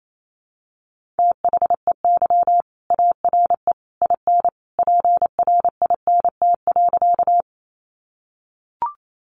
Morse code, heard as a keyed tone, about 28 wpm, 700 Hz. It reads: THEY ARE IN PRINT.
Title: Sentences from Top 1400 Words 28wpm